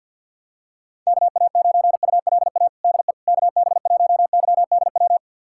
Morse code, heard as a keyed tone, about 52 wpm, 700 Hz. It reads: KA9ULA DE KB1YDW
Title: USA Callsign DE Callsign - 6 Characters - 50wpm